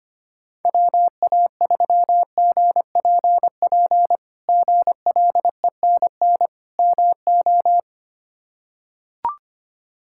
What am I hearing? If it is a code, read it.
Morse code, heard as a keyed tone, about 25 wpm, 700 Hz. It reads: WA3GPP GLENN MO
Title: K1USN SST QSO Practice - 25wpm